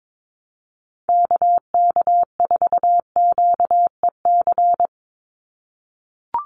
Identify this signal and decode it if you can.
Morse code, heard as a keyed tone, about 22 wpm, 700 Hz. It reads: KK4QEC
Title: USA Call Signs 22wpm